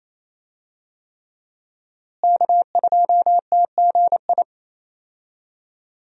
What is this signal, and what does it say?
Morse code, heard as a keyed tone, about 27 wpm, 700 Hz. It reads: K2TGI
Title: USA Call Signs - 5 Characters - Repeated 3x - 28wpm